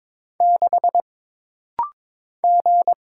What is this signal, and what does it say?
Morse code, heard as a keyed tone, about 22 wpm, 700 Hz. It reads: 6 G
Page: This rapid-fire timing is geared to push your ICR (Instant Character Recognition) skills.